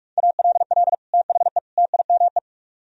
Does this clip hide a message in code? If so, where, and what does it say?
Yes: Morse code from 0.2 to 2.4 s, heard as a keyed tone, about 46 wpm, 700 Hz. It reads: ALL THE TIME